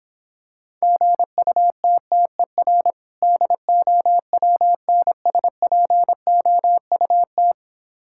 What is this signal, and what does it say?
Morse code, heard as a keyed tone, about 26 wpm, 700 Hz. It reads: GUTTER DOWNSPOUT